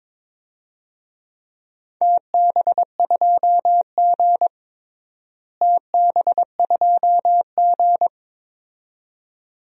Morse code, heard as a keyed tone, about 21 wpm, 700 Hz. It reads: TB2G TB2G
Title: DXCC-100 Call Signs - 4 Characters - Repeated 2x - 22wpm